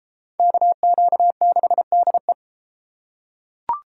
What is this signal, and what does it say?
Morse code, heard as a keyed tone, about 33 wpm, 700 Hz. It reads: KQ6DE